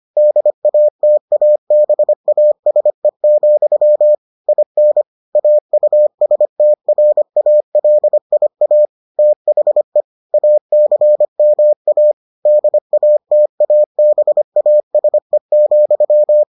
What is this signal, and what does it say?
Morse code, heard as a keyed tone, about 25 wpm, 600 Hz. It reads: DATABASE, IN AUSTRALIA THE ACMA DATABASE,